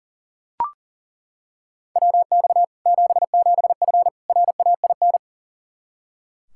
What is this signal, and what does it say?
Morse code, heard as a keyed tone, about 40 wpm, 700 Hz. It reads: WX 77F RAIN
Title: Practice QSO - Exchange 2 - Info - 40wpm